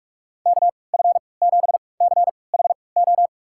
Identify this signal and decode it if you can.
Morse code, heard as a keyed tone, about 46 wpm, spaced as for 25 wpm, 700 Hz. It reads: KF7CHO
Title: USA Call Signs - 6 Characters - 3x Character Spacing - 45wpm